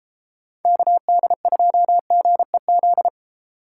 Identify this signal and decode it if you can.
Morse code, heard as a keyed tone, about 31 wpm, 700 Hz. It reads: KD2GEZ